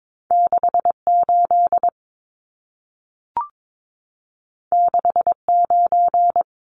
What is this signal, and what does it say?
Morse code, heard as a keyed tone, about 22 wpm, 700 Hz. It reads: 68 69
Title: Numbers - 2-Digit 22wpm